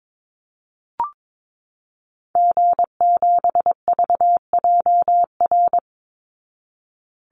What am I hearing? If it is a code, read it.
Morse code, heard as a keyed tone, about 22 wpm, 700 Hz. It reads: G7VJR
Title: Call Signs Repeated 3x - 22wpm